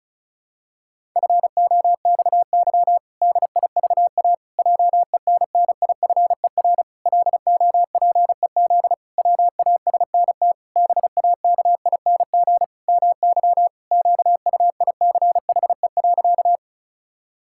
Morse code, heard as a keyed tone, about 34 wpm, 700 Hz. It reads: FOXY DIVA JENNIFER LOPEZ WASNT BAKING MY QUICHE.
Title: Pangram Challenge v2 - 35wpm